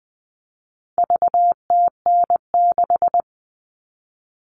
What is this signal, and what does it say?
Morse code, heard as a keyed tone, about 20 wpm, 700 Hz. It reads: VTN6
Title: All Cut Numbers - 4-Digit 20wpm